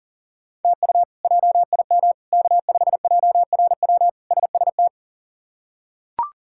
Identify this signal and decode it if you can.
Morse code, heard as a keyed tone, about 37 wpm, 700 Hz. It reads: TU JIM K5JRW SST
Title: K1USN SST QSO Practice - 40wpm